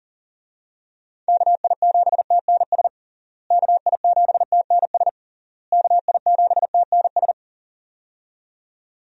Morse code, heard as a keyed tone, about 40 wpm, 700 Hz. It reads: KI7TNS KI7TNS KI7TNS
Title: USA Call Signs - 6 Characters - Repeated 3x - 40wpm